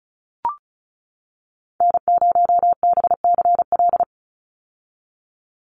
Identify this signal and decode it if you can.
Morse code, heard as a keyed tone, about 34 wpm, 700 Hz. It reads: N0BCL
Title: USA Call Signs 35wpm